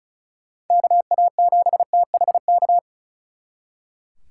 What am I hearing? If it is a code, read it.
Morse code, heard as a keyed tone, about 47 wpm, 700 Hz. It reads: KA7THK